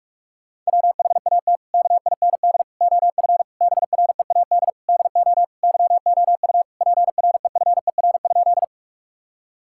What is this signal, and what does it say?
Morse code, heard as a keyed tone, about 41 wpm, 700 Hz. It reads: WHAT KIND OF BREAD DO YOU PREFER?
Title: Sentences from Top 1600 Words 45wpm